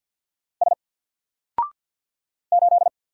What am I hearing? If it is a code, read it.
Morse code, heard as a keyed tone, about 48 wpm, 700 Hz. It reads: S 8
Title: Single Character (Letter:Number:Punctuation:Prosign) Alt 50wpm